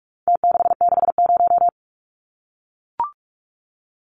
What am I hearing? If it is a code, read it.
Morse code, heard as a keyed tone, about 44 wpm, 700 Hz. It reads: T660
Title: Numbers - Contest Style - 4-Digit 45wpm